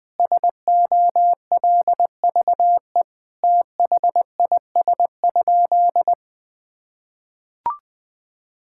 Morse code, heard as a keyed tone, about 20 wpm, 700 Hz. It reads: SOLVE THIS?
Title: Sentences with 5-Word Phrases - 20wpm